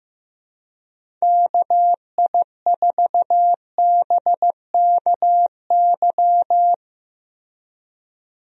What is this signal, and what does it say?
Morse code, heard as a keyed tone, about 15 wpm, 700 Hz. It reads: KI4BKY
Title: USA Call Signs - 6 Characters - Repeated 2x - 15wpm